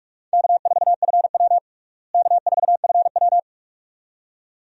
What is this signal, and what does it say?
Morse code, heard as a keyed tone, about 41 wpm, 700 Hz. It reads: K4FW K4FW
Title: USA Call Signs - 4 Characters - Repeated 2x - 45wpm